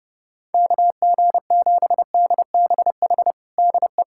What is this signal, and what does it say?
Morse code, heard as a keyed tone, about 30 wpm, 700 Hz. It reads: KG7DBH DE
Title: USA Callsign DE Callsign - 6 Characters - 30wpm